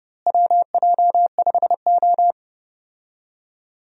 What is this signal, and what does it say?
Morse code, heard as a keyed tone, about 31 wpm, 700 Hz. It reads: WJ5O
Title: USA Call Signs - 4 Characters - Repeated 3x - 30wpm